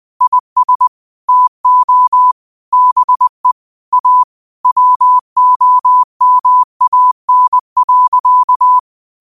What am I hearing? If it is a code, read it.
Morse code, heard as a keyed tone, about 20 wpm, 1000 Hz. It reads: IS TO BE A WOMAN.
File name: qotdfiles / 20wpm